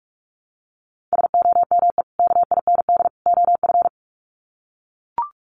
Morse code, heard as a keyed tone, about 44 wpm, 700 Hz. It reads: SOME KIND OF
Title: Sets of 3 Words 45wpm